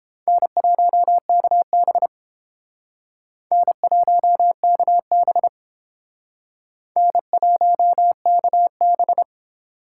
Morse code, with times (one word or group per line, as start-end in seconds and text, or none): 0.3-2.1 N1KB
3.5-5.5 N1KB
7.0-9.2 N1KB